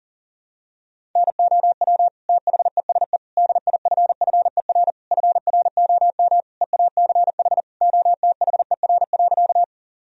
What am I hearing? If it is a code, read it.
Morse code, heard as a keyed tone, about 40 wpm, 700 Hz. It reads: NOW THESE DIFFER FROM EACH OTHER.
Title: Sentences from Top 1000 Words - Review Part 1 - 40wpm